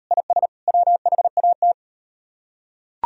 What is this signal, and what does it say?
Morse code, heard as a keyed tone, about 38 wpm, 700 Hz. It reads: IS WHAT